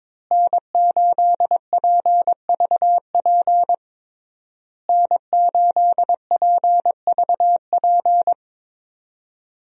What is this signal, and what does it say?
Morse code, heard as a keyed tone, about 22 wpm, 700 Hz. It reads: N8PVP N8PVP
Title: USA Call Signs - 5 Characters - Repeated 3x - 22wpm